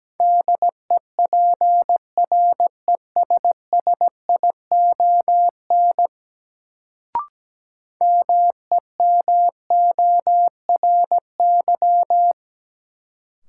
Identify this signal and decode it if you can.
Morse code, heard as a keyed tone, about 17 wpm, 700 Hz. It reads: DEPRESSION MEMORY
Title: Sentences from Top 1300 Words - New Word Drill Alt - 17wpm